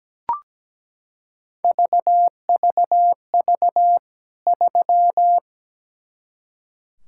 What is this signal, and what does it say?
Morse code, heard as a keyed tone, about 17 wpm, 700 Hz. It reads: VVV 3
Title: K1USN SST QSO Practice 17wpm